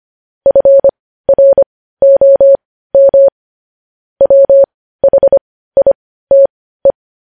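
Morse code, heard as a keyed tone, about 25 wpm, spaced as for 15 wpm, 550 Hz. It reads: FROM WHITE
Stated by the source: Morse code news headlines sent at 15 WPM.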